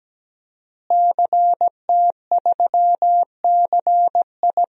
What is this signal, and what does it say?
Morse code, heard as a keyed tone, about 17 wpm, 700 Hz. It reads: CT3CI
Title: DXCC-100 Call Signs - 5 Characters - Repeated 3x - 17wpm